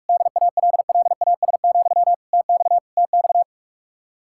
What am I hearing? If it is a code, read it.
Morse code, heard as a keyed tone, about 47 wpm, 700 Hz. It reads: DALLAS, TX TX